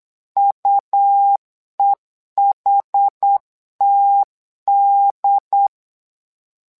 Morse code, heard as a keyed tone, about 8 wpm, 800 Hz. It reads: UEHTD
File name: LEZIONI / 2 UHDN / 2 UHD 8wpm 2258